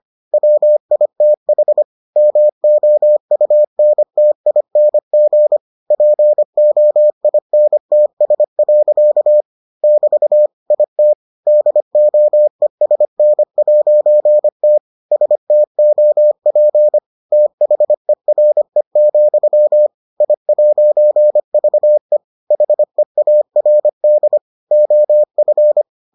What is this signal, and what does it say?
Morse code, heard as a keyed tone, about 25 wpm, 600 Hz. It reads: WITH MOUNTING POINTS. <BT> IT DOESN'T STOP THERE, I'VE HEARD OF